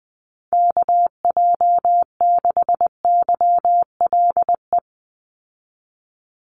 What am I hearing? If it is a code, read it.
Morse code, heard as a keyed tone, about 20 wpm, 700 Hz. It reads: KJ6YLE
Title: USA Call Signs 20wpm